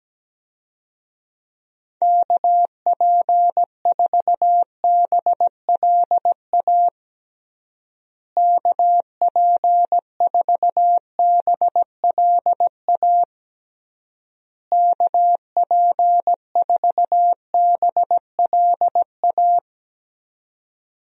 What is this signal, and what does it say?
Morse code, heard as a keyed tone, about 17 wpm, 700 Hz. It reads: KP4BLA KP4BLA KP4BLA
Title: DXCC-100 Call Signs - 6 Characters - Repeated 3x - 17wpm